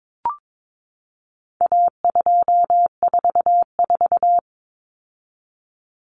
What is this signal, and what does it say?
Morse code, heard as a keyed tone, about 22 wpm, 700 Hz. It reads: A244
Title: Common Cut Numbers - 4-Digit 22wpm